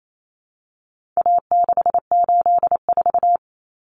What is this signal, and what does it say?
Morse code, heard as a keyed tone, about 28 wpm, 700 Hz. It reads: A684
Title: Common Cut Numbers - 4-Digit 28wpm